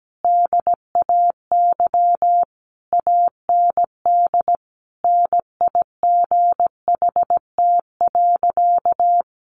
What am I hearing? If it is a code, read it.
Morse code, heard as a keyed tone, about 17 wpm, 700 Hz. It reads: DAY AND NIGHT.
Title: Sentences from Top 1500 Words 17wpm